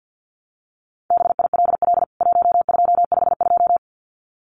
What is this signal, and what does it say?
Morse code, heard as a keyed tone, about 50 wpm, 700 Hz. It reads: BILL 1252
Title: CWT Exchange 50wpm